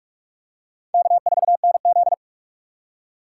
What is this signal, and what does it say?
Morse code, heard as a keyed tone, about 38 wpm, 700 Hz. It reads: K4NZ